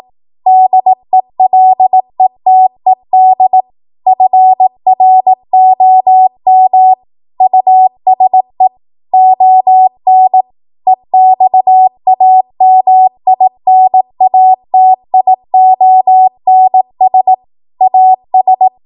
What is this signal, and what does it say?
Morse code, heard as a keyed tone, about 18 wpm, 750 Hz. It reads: DELETED FROM USE ON EXAMINATIONS AS